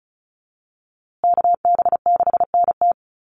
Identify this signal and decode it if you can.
Morse code, heard as a keyed tone, about 34 wpm, 700 Hz. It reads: KB6NT
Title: USA Call Signs Repeated 3x 35wpm